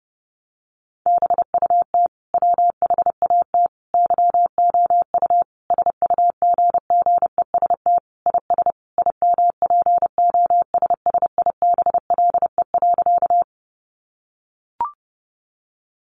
Morse code, heard as a keyed tone, about 30 wpm, 700 Hz. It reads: BUT WHAT YOU SUGGEST IS IMPOSSIBLE.
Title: Sentences from Top 1300 Words 30wpm